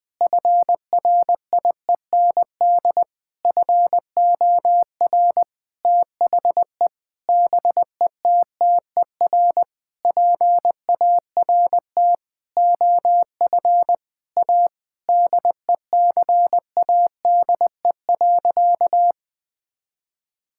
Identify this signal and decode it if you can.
Morse code, heard as a keyed tone, about 20 wpm, 700 Hz. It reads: FRIEND FOR THE BETTER PART OF A DECADE.
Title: Sentences with 6-Word Phrases - 20wpm